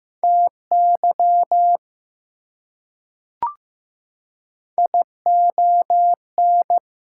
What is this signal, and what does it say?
Morse code, heard as a keyed tone, about 15 wpm, 700 Hz. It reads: TY ION
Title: Prefixes and Suffixes 15wpm